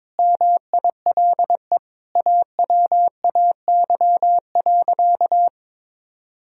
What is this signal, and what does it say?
Morse code, heard as a keyed tone, about 22 wpm, 700 Hz. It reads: MILE AWAY.
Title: Sentences from Top 1000 Words - Review Part 1 - 22wpm